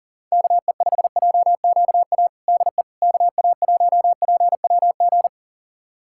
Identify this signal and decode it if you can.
Morse code, heard as a keyed tone, about 40 wpm, 700 Hz. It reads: KE5JQA DE KA1PWG
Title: USA Callsign DE Callsign 40wpm